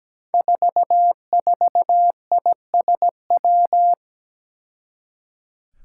Morse code, heard as a keyed tone, about 17 wpm, 700 Hz. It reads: 44ISW
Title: Five Challenging Letter-Number Combinations 17wpm